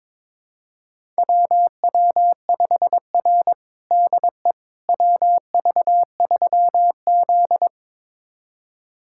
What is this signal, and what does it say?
Morse code, heard as a keyed tone, about 22 wpm, 700 Hz. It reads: WW5R DE WV3Z
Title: USA Callsign DE Callsign - 4 Characters - 22wpm